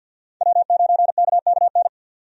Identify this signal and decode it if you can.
Morse code, heard as a keyed tone, about 50 wpm, 700 Hz. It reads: W9KKN
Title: USA Call Signs - 5 Characters - 50wpm